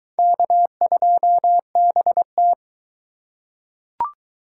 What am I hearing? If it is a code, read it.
Morse code, heard as a keyed tone, about 21 wpm, 700 Hz. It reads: K2BT